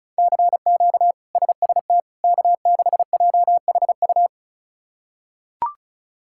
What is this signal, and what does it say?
Morse code, heard as a keyed tone, about 36 wpm, 700 Hz. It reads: CQ SST K6JHU
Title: K1USN SST QSO Practice - 35wpm